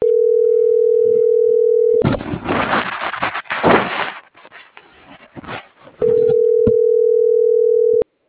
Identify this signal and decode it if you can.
Ringback tone (North America), 440 + 480 Hz.
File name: CENTRAL MABE / out cristopher.carrera